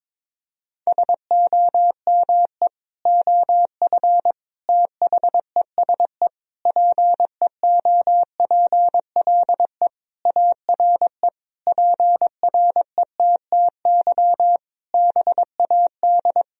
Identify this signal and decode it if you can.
Morse code, heard as a keyed tone, about 22 wpm, 700 Hz. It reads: SOME OF THESE PEOPLE ARE PRETTY BAD
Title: Sentences from Top 1000 Words - Review Part 3 - 22wpm